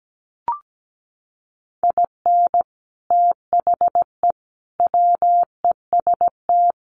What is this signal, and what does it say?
Morse code, heard as a keyed tone, about 17 wpm, 700 Hz. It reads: IN THE WEST
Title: Sets of 5 Words Encore2 17wpm